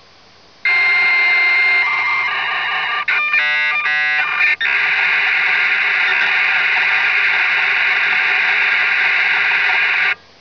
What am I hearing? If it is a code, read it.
Dial-up modem handshake.